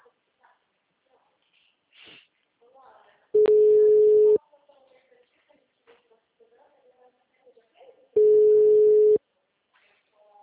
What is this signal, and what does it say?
Ringback tone (Europe, 425 Hz), with speech.